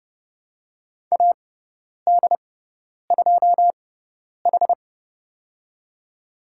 Morse code, heard as a keyed tone, about 31 wpm, spaced as for 10 wpm, 700 Hz. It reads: AD2H
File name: Lesson 17g - Letter u - Call Signs 30wpm